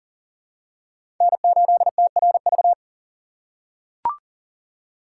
Morse code, heard as a keyed tone, about 40 wpm, 700 Hz. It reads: N8TRV
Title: USA Call Signs - 5 Characters - 4x Character Spacing - 40wpm